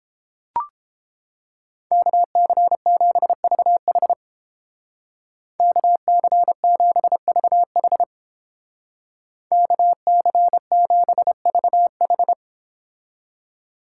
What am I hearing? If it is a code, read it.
Morse code, heard as a keyed tone, about 29 wpm, 700 Hz. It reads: KC7VH KC7VH KC7VH